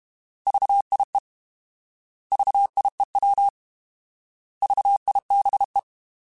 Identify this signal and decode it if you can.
Morse code, heard as a keyed tone, about 32 wpm, 800 Hz. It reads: VIE VIEW VIBE
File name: Alphabetical / Alphabetical mp3s / Vs @30